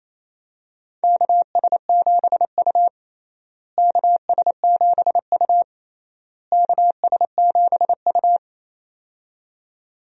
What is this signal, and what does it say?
Morse code, heard as a keyed tone, about 28 wpm, 700 Hz. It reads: KS7U KS7U KS7U
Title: USA Call Signs - 4 Characters - Repeated 3x - 28wpm